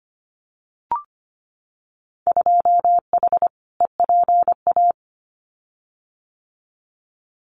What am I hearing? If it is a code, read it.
Morse code, heard as a keyed tone, about 25 wpm, 700 Hz. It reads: 2H EPA
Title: Winter Field Day Exchanges Encore 25wpm